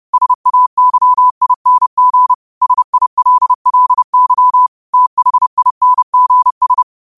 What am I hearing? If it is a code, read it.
Morse code, heard as a keyed tone, about 30 wpm, 1000 Hz. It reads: SAYING SILLY THINGS
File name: qotdfiles / 30wpm